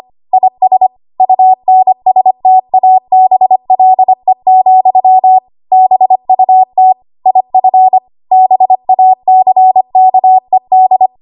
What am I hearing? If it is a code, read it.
Morse code, heard as a keyed tone, about 25 wpm, 750 Hz. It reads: IS UNSTABLE, BUT IF BACKED
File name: Archive / 25 WPM / 25WPM